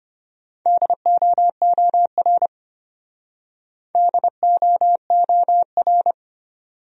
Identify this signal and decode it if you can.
Morse code, heard as a keyed tone, about 33 wpm, 700 Hz. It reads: DOOR DOOR